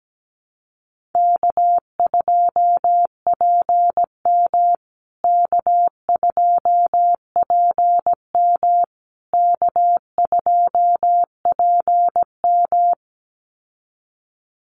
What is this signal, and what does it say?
Morse code, heard as a keyed tone, about 17 wpm, 700 Hz. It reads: K2PM K2PM K2PM
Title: USA Call Signs Repeated 3x 17wpm